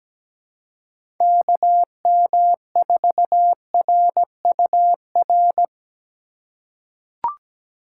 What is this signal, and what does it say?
Morse code, heard as a keyed tone, about 17 wpm, 700 Hz. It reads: KM4RUR
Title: USA Call Signs - 6 Characters - 3x Character Spacing - 17wpm